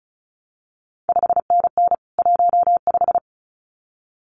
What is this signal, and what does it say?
Morse code, heard as a keyed tone, about 36 wpm, 700 Hz. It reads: 5NN 15
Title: CQ WW DX Contest Exchange Encore - 35wpm